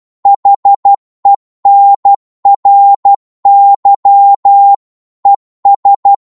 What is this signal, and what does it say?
Morse code, heard as a keyed tone, about 12 wpm, 800 Hz. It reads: HENRY ES
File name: Morse / qso-12wpm